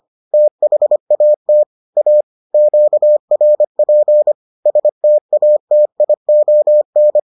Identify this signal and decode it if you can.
Morse code, heard as a keyed tone, about 25 wpm, 600 Hz. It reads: THAT A QRP STATION